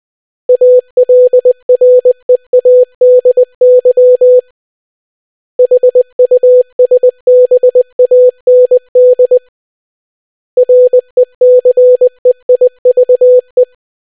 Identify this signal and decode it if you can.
Morse code, heard as a keyed tone, about 20 wpm, 500 Hz. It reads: ALREADY HUSBAND RECEIVE